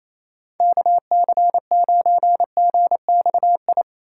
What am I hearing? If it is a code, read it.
Morse code, heard as a keyed tone, about 34 wpm, 700 Hz. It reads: KC9GXI
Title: Speed Racing - USA Call Signs 28wpm